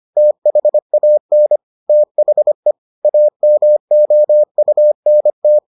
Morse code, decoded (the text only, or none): THAN THE AMOUNT